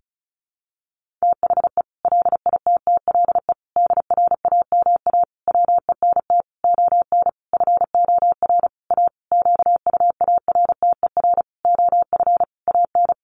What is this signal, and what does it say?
Morse code, heard as a keyed tone, about 34 wpm, 700 Hz. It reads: THE LITTLE DRAMA WENT ON FOR A QUARTER OF AN
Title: Sentences from Top 1500 Words 35wpm